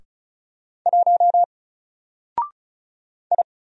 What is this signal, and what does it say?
Morse code, heard as a keyed tone, about 34 wpm, 700 Hz. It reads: 1 I
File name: Single-Letter-Number-35wpm